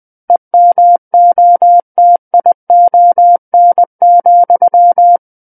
Morse code, heard as a keyed tone, about 20 wpm, 700 Hz. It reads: EMOTION,
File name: 20 WPM